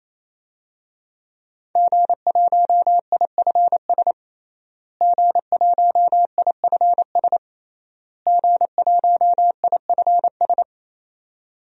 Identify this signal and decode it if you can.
Morse code, heard as a keyed tone, about 28 wpm, 700 Hz. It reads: G1IFS G1IFS G1IFS
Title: DXCC-100 Call Signs - 5 Characters - Repeated 3x - 28wpm